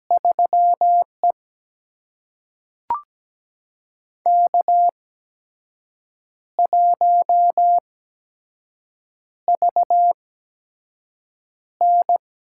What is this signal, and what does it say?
Morse code, heard as a keyed tone, about 17 wpm, spaced as for 6 wpm, 700 Hz. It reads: 3E K1VN
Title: USA Call Signs - 4 Characters - 8x Character Spacing - 17wpm